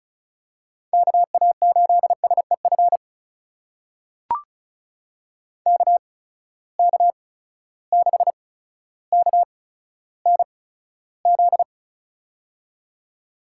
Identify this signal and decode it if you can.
Morse code, heard as a keyed tone, about 36 wpm, spaced as for 15 wpm, 700 Hz. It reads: KA8SEF KK6KNZ
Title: USA Call Signs - 6 Characters - 8x Character Spacing - 35wpm